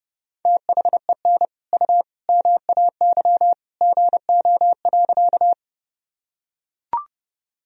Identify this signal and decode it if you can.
Morse code, heard as a keyed tone, about 30 wpm, 700 Hz. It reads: THEN U MAY GO.
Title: Sentences from Top 100 Words and Abbreviations 30wpm